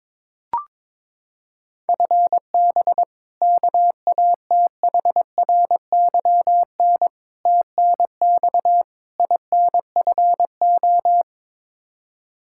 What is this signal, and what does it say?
Morse code, heard as a keyed tone, about 21 wpm, 700 Hz. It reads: FB KATHRYN TNX INFO
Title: Practice QSO - Exchange 3 - More Info - 22wpm